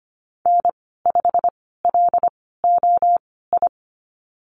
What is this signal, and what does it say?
Morse code, heard as a keyed tone, about 25 wpm, spaced as for 16 wpm, 690 Hz. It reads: N5LOI
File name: Lesson 15g - Number 5 - Call Signs 25wpm